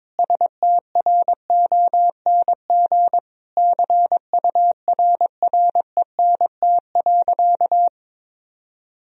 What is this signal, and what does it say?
Morse code, heard as a keyed tone, about 22 wpm, 700 Hz. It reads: STRONG CURRENT.
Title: Sentences from Top 1000 Words - Review Part 4 - 22wpm